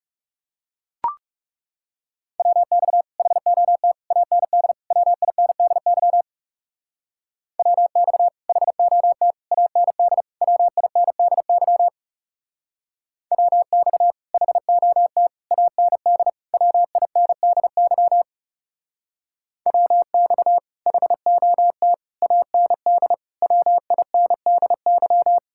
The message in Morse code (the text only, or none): WX HOT AND WINDY WX HOT AND WINDY WX HOT AND WINDY WX HOT AND WINDY